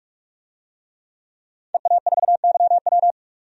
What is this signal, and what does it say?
Morse code, heard as a keyed tone, about 44 wpm, 700 Hz. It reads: EA4YW
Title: DXCC-100 Call Signs - 5 Characters - 45wpm